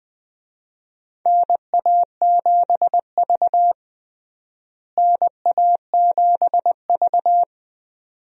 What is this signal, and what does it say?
Morse code, heard as a keyed tone, about 20 wpm, 700 Hz. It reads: NA7V NA7V
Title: USA Call Signs - 4 Characters - Repeated 2x - 20wpm